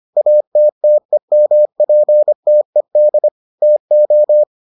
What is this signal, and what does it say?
Morse code, heard as a keyed tone, about 25 wpm, 600 Hz. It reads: ATTEMPTED TO